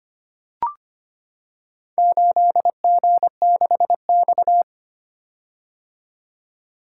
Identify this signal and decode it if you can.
Morse code, heard as a keyed tone, about 25 wpm, 700 Hz. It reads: 8G6X
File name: practice / 25wpm / Four Letter-Number Combinations - 25wpm